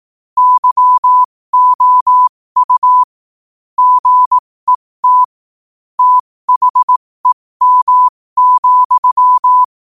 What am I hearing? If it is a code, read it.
Morse code, heard as a keyed tone, about 18 wpm, spaced as for 15 wpm, 1000 Hz. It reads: YOU GET THEM,
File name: qotdfiles / 15wpm